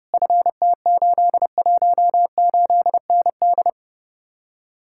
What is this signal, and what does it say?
Morse code, heard as a keyed tone, about 37 wpm, 700 Hz. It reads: FT818ND
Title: Speed-Racing Common QSO Phrases - 30wpm